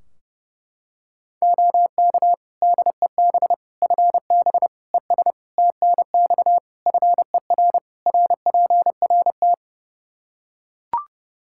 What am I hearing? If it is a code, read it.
Morse code, heard as a keyed tone, about 30 wpm, 700 Hz. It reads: OK DEB FB ES TNX FER RPRT
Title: Practice QSO - Exchange 2 - Info - 30wpm